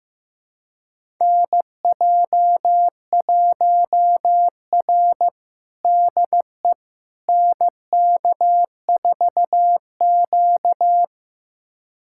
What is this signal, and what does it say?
Morse code, heard as a keyed tone, about 15 wpm, 700 Hz. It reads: NJ1R DE NK4Q
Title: USA Callsign DE Callsign - 4 Characters - 15wpm